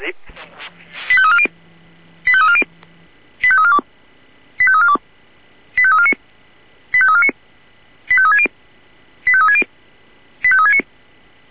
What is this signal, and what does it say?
Five-tone selective call (ZVEI), heard as quick runs of five tones, tones of about 70 ms, with speech. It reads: 85340 85340 85342 85342 85349 85349 85360 85360 85369
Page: /496/SDR/ZVEI/85342/